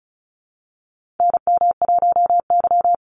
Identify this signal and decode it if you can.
Morse code, heard as a keyed tone, about 34 wpm, 700 Hz. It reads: NM1Y
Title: Call Signs - 35wpm